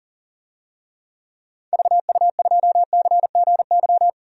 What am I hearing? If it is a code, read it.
Morse code, heard as a keyed tone, about 40 wpm, 700 Hz. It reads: VU2CGY
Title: DXCC-100 Call Signs - 6 Characters - 40wpm